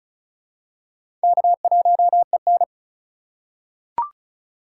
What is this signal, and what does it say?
Morse code, heard as a keyed tone, about 36 wpm, 700 Hz. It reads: K1EN
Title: USA Call Signs - 4x Character Spacing - 35wpm